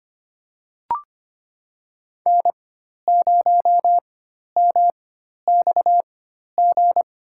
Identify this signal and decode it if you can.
Morse code, heard as a keyed tone, about 25 wpm, spaced as for 12 wpm, 700 Hz. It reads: N0MXG
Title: USA Call Signs - 4x Character Spacing - 25wpm